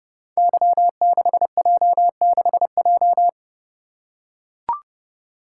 Morse code, heard as a keyed tone, about 30 wpm, 700 Hz. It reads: Y6J6J
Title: Code Groups - 5 Letter:Numbers - 30wpm